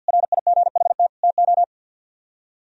Morse code, heard as a keyed tone, about 60 wpm, 700 Hz. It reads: RIGHT TO
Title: Speed-Racing - Sets of 3 Words 50wpm